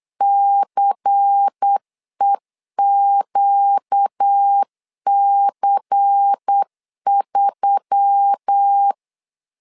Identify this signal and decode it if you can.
Morse code, heard as a keyed tone, about 8 wpm, 800 Hz. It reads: CEQC3
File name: LEZIONI / 5 LBVC / 5 LBVC 8wpm 2935